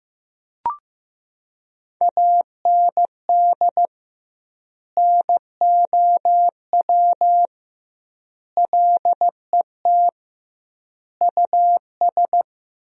Morse code, heard as a keyed tone, about 15 wpm, 700 Hz. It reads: AND NOW LET US
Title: 2x Word-Spacing - Sentences from Top 300 Words - 15wpm